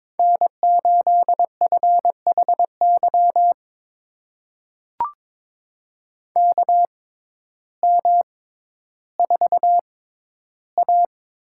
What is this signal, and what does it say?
Morse code, heard as a keyed tone, about 22 wpm, spaced as for 12 wpm, 700 Hz. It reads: N8FHY KM4A
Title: USA Call Signs - 5 Characters - 6x Character Spacing - 22wpm